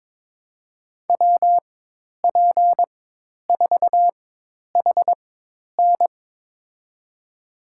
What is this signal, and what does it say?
Morse code, heard as a keyed tone, about 22 wpm, spaced as for 10 wpm, 700 Hz. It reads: WP4HN
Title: USA Call Signs - 5 Characters - 4x Character Spacing - 22wpm